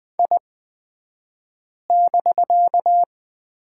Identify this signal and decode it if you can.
Morse code, heard as a keyed tone, about 20 wpm, spaced as for 5 wpm, 700 Hz. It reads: I<BK>
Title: 249 Lesson 40d - BK - Two Characters 20wpm